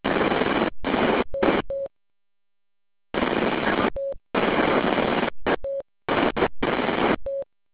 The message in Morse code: none